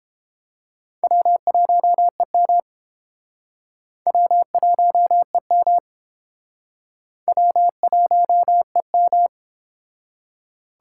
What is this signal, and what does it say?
Morse code, heard as a keyed tone, about 29 wpm, 700 Hz. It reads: W1EM W1EM W1EM